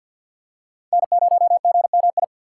Morse code, heard as a keyed tone, about 53 wpm, 700 Hz. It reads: N0GMI
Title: USA Call Signs - 5 Characters - 6x Character Spacing - 50wpm